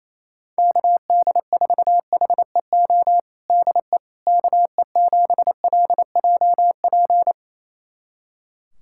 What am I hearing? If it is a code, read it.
Morse code, heard as a keyed tone, about 28 wpm, 700 Hz. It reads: KD4HEO DE KE7LJP
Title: USA Callsign DE Callsign 28wpm